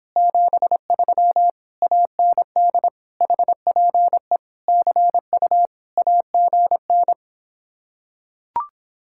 Morse code, heard as a keyed tone, about 26 wpm, 700 Hz. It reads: 73 AND HPE CU AGN